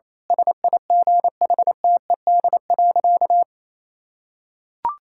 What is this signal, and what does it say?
Morse code, heard as a keyed tone, about 28 wpm, 700 Hz. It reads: SIGHTED.
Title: Pangram Challenge v2 - 28wpm